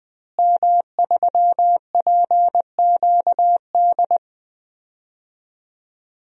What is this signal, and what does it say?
Morse code, heard as a keyed tone, about 20 wpm, 700 Hz. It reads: M3PQD
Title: DXCC-100 Call Signs - 5 Characters - Repeated 3x - 20wpm